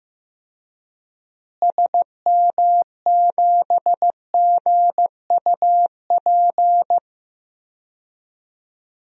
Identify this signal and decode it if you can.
Morse code, heard as a keyed tone, about 15 wpm, 700 Hz. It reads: SM7GUP
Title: DXCC-100 Call Signs - 6 Characters - Repeated 2x - 15wpm